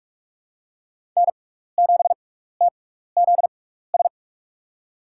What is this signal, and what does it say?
Morse code, heard as a keyed tone, about 44 wpm, spaced as for 15 wpm, 700 Hz. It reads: N7TZS
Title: USA Call Signs - 5 Characters - 6x Character Spacing - 45wpm